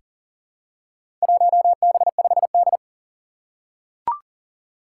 Morse code, heard as a keyed tone, about 40 wpm, 700 Hz. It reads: 1B5D
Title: Four Challenging Letter-Number Combinations 40wpm